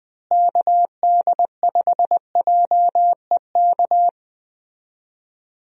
Morse code, heard as a keyed tone, about 21 wpm, 700 Hz. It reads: KD5JEK
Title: USA Call Signs - 6 Characters - Repeated 3x - 20wpm